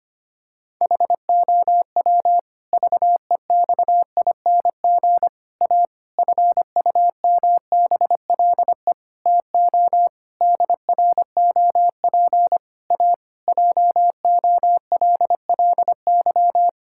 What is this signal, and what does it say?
Morse code, heard as a keyed tone, about 25 wpm, 700 Hz. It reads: HOW VEXING A FUMBLE TO DROP A JOLLY
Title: Pangram Challenge v2 - 25wpm